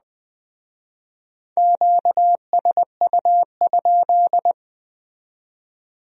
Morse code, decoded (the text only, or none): QSU?